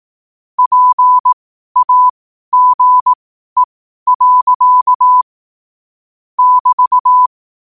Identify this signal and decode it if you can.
Morse code, heard as a keyed tone, about 18 wpm, spaced as for 12 wpm, 1000 Hz. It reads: PAGE. <BT>